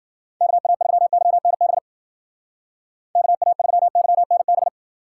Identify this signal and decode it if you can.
Morse code, heard as a keyed tone, about 48 wpm, 700 Hz. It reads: KA3YNB KA3YNB
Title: Speed Racing - USA 2x3 Call Signs 40wpm